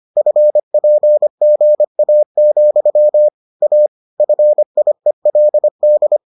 Morse code, heard as a keyed tone, about 25 wpm, 600 Hz. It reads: FPGA, A FIELD